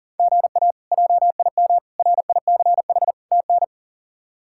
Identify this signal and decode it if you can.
Morse code, heard as a keyed tone, about 40 wpm, 700 Hz. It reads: GA JIM RICH TN
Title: K1USN SST QSO Practice - 40wpm